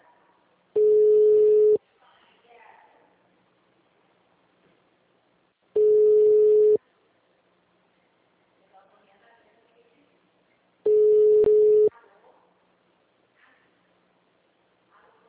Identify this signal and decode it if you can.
Ringback tone (Europe, 425 Hz), with speech.
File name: out- -2004- - - .12238